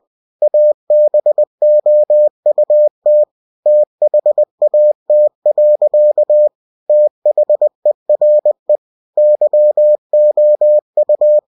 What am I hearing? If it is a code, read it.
Morse code, heard as a keyed tone, about 20 wpm, 600 Hz. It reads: ABOUT THAT. THERE YOU
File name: cw-books / winnie-the-pooh / winnie-20wpm-page-0037